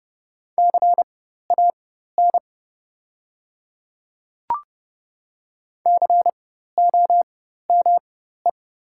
Morse code, heard as a keyed tone, about 30 wpm, spaced as for 14 wpm, 700 Hz. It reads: CAN COME
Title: 4x Character Spacing - Top 100 Words - 30wpm